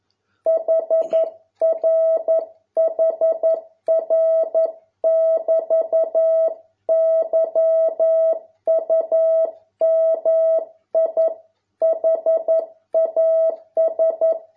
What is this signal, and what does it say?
Morse code, heard as a keyed tone, about 11 wpm, 650 Hz. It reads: HR HR <BT> YUMI HAS